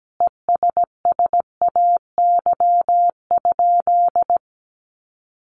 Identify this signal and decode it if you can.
Morse code, heard as a keyed tone, about 17 wpm, 700 Hz. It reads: ESSAY?
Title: Sentences from Top 1500 Words 17wpm